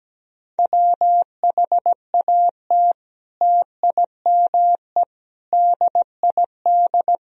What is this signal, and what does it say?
Morse code, heard as a keyed tone, about 17 wpm, 700 Hz. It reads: WHAT TIME DID
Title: Sentences from Top 1000 Words - Review Part 1 - 17wpm